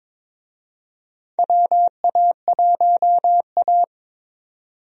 Morse code, heard as a keyed tone, about 22 wpm, 700 Hz. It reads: WA1A
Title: USA Call Signs - 4 Characters - 22wpm